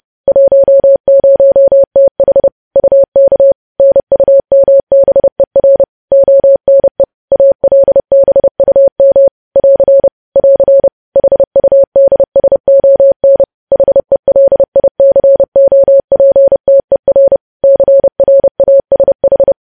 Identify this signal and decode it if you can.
Morse code, heard as a keyed tone, about 30 wpm, 550 Hz. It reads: 10TH UK NUMBER ONE ALBUM <AR> <AR> HUDSON HELICOPTER CRASH
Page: news headlines in morse code at 30 wpm